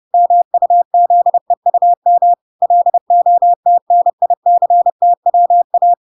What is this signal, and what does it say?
Morse code, heard as a keyed tone, about 30 wpm, 700 Hz. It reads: MUZEUM LOTNICTWA